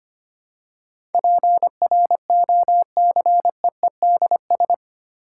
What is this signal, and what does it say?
Morse code, heard as a keyed tone, about 25 wpm, 700 Hz. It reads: PROCEEDS
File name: 8 Letter Words - First 3 Letters Given - 25wpm